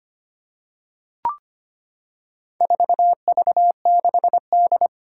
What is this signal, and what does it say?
Morse code, heard as a keyed tone, about 25 wpm, 700 Hz. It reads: 4V6D
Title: Four Challenging Letter-Number Combinations 25wpm